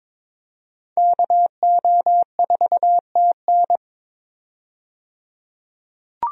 Morse code, heard as a keyed tone, about 21 wpm, 700 Hz. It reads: KO4TN
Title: USA Call Signs - 5 Characters - 22wpm